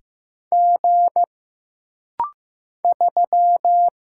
Morse code, heard as a keyed tone, about 15 wpm, 700 Hz. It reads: G 3
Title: Single Letter-Number Alt - Rapid-Fire 15wpm